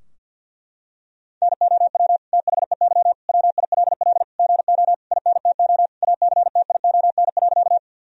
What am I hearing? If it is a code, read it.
Morse code, heard as a keyed tone, about 47 wpm, 700 Hz. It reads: NOW THEY WILL GO INTO ACTION.
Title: Sentences from Top 1000 Words - Review Part 4 - 50wpm